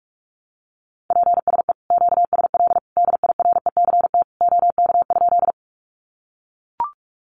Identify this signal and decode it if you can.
Morse code, heard as a keyed tone, about 44 wpm, 700 Hz. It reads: PSE QSL DIRECT OK?
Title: Common QSO Phrases 45wpm